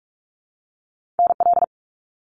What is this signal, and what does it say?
Morse code, heard as a keyed tone, about 46 wpm, 700 Hz. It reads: NL
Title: Canadian Provinces and Territories 45wpm